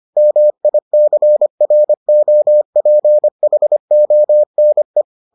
Morse code, heard as a keyed tone, about 25 wpm, 600 Hz. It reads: MICROPHONE